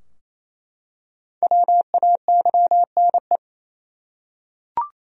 Morse code, heard as a keyed tone, about 27 wpm, 700 Hz. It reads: WAYNE
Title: USA Common Names - Max 5 Letters - 28wpm